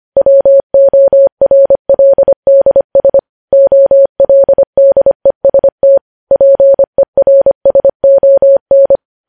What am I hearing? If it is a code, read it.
Morse code, heard as a keyed tone, about 25 wpm, 550 Hz. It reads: WORLDS OLDEST PERSON
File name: podcast / News headlines-1x-25x25